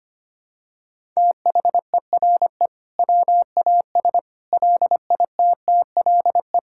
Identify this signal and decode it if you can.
Morse code, heard as a keyed tone, about 25 wpm, 700 Hz. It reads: THERE WAS LITTLE
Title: Sentences from Top 1000 Words - Review Part 5 - 25wpm